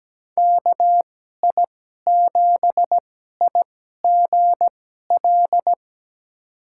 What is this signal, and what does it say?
Morse code, heard as a keyed tone, about 17 wpm, spaced as for 12 wpm, 700 Hz. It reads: KI7IGL
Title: USA Call Signs - 6 Characters - 2x Character Spacing - 17wpm